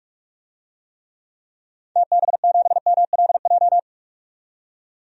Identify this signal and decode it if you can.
Morse code, heard as a keyed tone, about 46 wpm, 700 Hz. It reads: TB7MLJ
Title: DXCC-100 Call Signs - 6 Characters - 45wpm